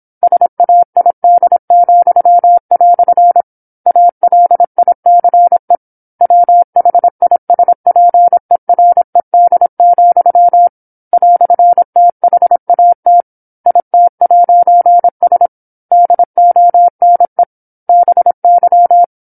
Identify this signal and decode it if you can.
Morse code, heard as a keyed tone, about 27 wpm, 700 Hz. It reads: SAID," ALICE WHISPERED, "THAT IT'S DONE BY
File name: 08-Chapter-08 26 WPM